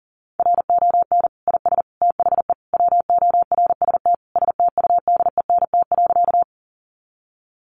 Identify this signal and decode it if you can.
Morse code, heard as a keyed tone, about 40 wpm, 700 Hz. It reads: RON IS THE WORST STUDENT.
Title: Additional Easy Phrases 40wpm